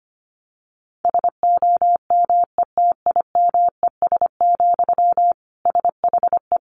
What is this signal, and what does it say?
Morse code, heard as a keyed tone, about 25 wpm, 700 Hz. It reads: SOMETIMES, SHE
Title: Sentences from Top 1000 Words 25wpm